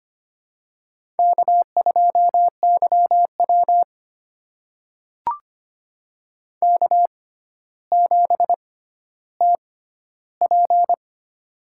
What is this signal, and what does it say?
Morse code, heard as a keyed tone, about 25 wpm, spaced as for 13 wpm, 700 Hz. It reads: K2YW K7TP
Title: USA Call Signs - 4 Characters - 6x Character Spacing - 25wpm